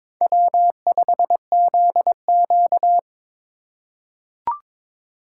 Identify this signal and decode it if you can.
Morse code, heard as a keyed tone, about 22 wpm, 700 Hz. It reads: W5ZQ
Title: USA Call Signs - 4 Characters - 2x Character Spacing - 22wpm